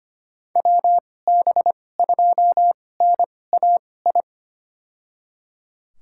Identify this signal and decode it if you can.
Morse code, heard as a keyed tone, about 25 wpm, spaced as for 18 wpm, 700 Hz. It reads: WB2NAI
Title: USA Call Signs - 2x Character Spacing - 25wpm